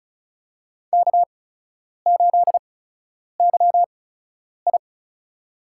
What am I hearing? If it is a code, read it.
Morse code, heard as a keyed tone, about 36 wpm, spaced as for 10 wpm, 700 Hz. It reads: K8YI
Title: USA Call Signs - 4 Characters - 8x Character Spacing - 35wpm